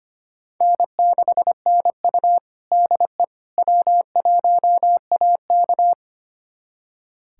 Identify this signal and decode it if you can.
Morse code, heard as a keyed tone, about 25 wpm, 700 Hz. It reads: N6NU DE W1AK
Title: USA Callsign DE Callsign - 4 Characters - 25wpm